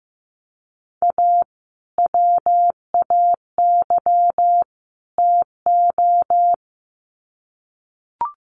Morse code, heard as a keyed tone, about 15 wpm, 700 Hz. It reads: A WAY TO
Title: Sets of 3 Words 15wpm